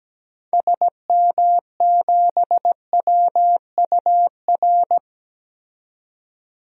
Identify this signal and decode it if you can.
Morse code, heard as a keyed tone, about 17 wpm, 700 Hz. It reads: SM7WUR
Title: DXCC-100 Call Signs - 6 Characters - 17wpm